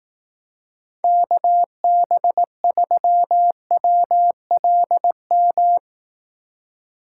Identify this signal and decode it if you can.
Morse code, heard as a keyed tone, about 18 wpm, 700 Hz. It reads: KB3WLM
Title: Speed Racing - USA 2x3 Call Signs 15wpm